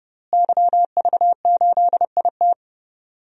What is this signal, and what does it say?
Morse code, heard as a keyed tone, about 30 wpm, 700 Hz. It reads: YV8IT
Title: DXCC-100 Call Signs - 5 Characters - Repeated 3x - 30wpm